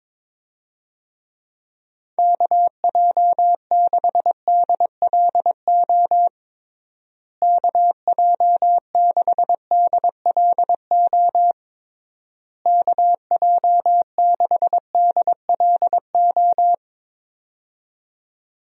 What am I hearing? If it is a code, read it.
Morse code, heard as a keyed tone, about 22 wpm, 700 Hz. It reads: KJ6DLO KJ6DLO KJ6DLO
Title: USA Call Signs - 6 Characters - Repeated 3x - 22wpm